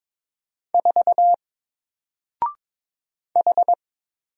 Morse code, heard as a keyed tone, about 22 wpm, 700 Hz. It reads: 4 H
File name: Single-Letter-Number-22wpm